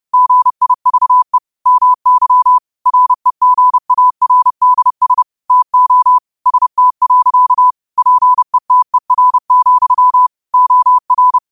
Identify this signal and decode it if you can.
Morse code, heard as a keyed tone, about 30 wpm, 1000 Hz. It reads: GIVE MY REGARDS TO ST. PETER, OR